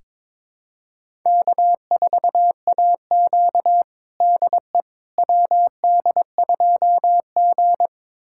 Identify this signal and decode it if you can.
Morse code, heard as a keyed tone, about 22 wpm, 700 Hz. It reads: K4AQ DE WD2G
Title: USA Callsign DE Callsign - 4 Characters - 22wpm